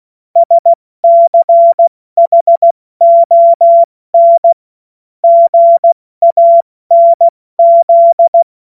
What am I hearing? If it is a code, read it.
Morse code, heard as a keyed tone, about 16 wpm, 680 Hz. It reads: SCHON GANZ
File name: Morsekurs / Witze / Witz 21. Dezember 2021, 70 F80